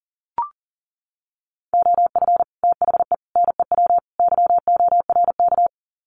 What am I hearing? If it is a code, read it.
Morse code, heard as a keyed tone, about 40 wpm, 700 Hz. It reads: OF THE NEW YORK
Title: Sets of 4 Words 40wpm